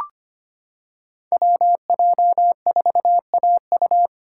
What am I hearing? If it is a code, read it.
Morse code, heard as a keyed tone, about 25 wpm, 700 Hz. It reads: WJ4AU
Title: Code Groups - 5 Letter:Numbers - 25wpm